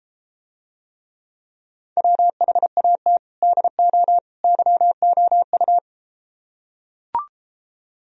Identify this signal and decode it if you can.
Morse code, heard as a keyed tone, about 34 wpm, 700 Hz. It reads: WHAT DO YOU